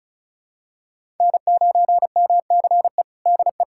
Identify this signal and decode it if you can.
Morse code, heard as a keyed tone, about 36 wpm, 700 Hz. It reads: N9MCE DE
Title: USA Callsign DE Callsign - 5 Characters - 35wpm